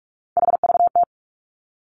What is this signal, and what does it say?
Morse code, heard as a keyed tone, about 46 wpm, 700 Hz. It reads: HVT
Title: Three Challenging Letter-Number Combinations 45wpm